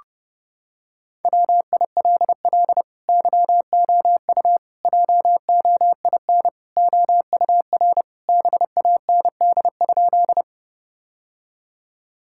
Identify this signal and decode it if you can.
Morse code, heard as a keyed tone, about 30 wpm, 700 Hz. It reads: WILL YOU JOIN OUR BAND?
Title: Sentences from Top 1600 Words 30wpm